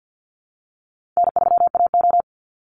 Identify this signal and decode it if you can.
Morse code, heard as a keyed tone, about 50 wpm, 700 Hz. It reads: N3AO
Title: Easy Call Signs - 50wpm